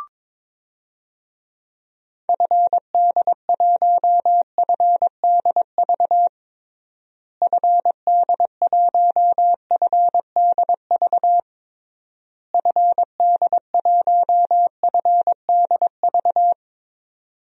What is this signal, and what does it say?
Morse code, heard as a keyed tone, about 22 wpm, 700 Hz. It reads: FD1FDV FD1FDV FD1FDV
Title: DXCC-100 Call Signs - 6 Characters - Repeated 3x - 22wpm